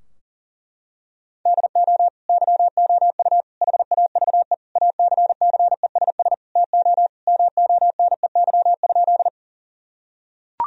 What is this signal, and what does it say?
Morse code, heard as a keyed tone, about 40 wpm, 700 Hz. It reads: DO YOU HAVE ACCESS TO MONEY?
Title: Sentences from Top 1000 Words - Review Part 4 - 40wpm